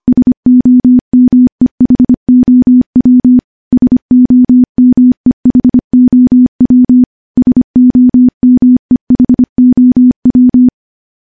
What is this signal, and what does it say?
Morse code, heard as a keyed tone, about 25 wpm, 250 Hz. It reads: SOMEHOW SOMEHOW SOMEHOW
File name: morse-training-25WPM